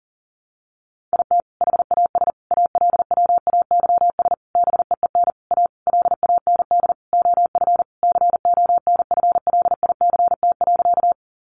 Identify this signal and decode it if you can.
Morse code, heard as a keyed tone, about 40 wpm, 700 Hz. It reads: IT HAS ALWAYS BEEN A LAND OF CONFLICT.
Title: Sentences from Top 1300 Words 40wpm